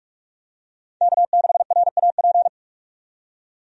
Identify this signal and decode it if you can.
Morse code, heard as a keyed tone, about 47 wpm, 700 Hz. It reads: K6RAP
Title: USA Call Signs - 5 Characters - 45wpm